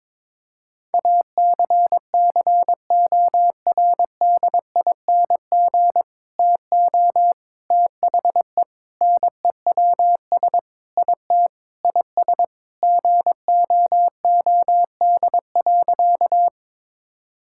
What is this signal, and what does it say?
Morse code, heard as a keyed tone, about 22 wpm, 700 Hz. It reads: ACCORDING TO THE NEWS IT IS GOOD.
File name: practice / 22wpm / Sentences with 3-Word Phrases - 22wpm